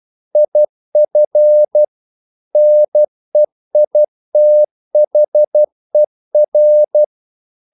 Morse code, heard as a keyed tone, about 12 wpm, 600 Hz. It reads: IF NEITHER
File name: CW / Practice / If12